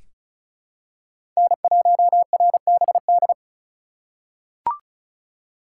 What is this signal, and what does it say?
Morse code, heard as a keyed tone, about 36 wpm, 700 Hz. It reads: N1RBD
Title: K1USN SST QSO Practice - 35wpm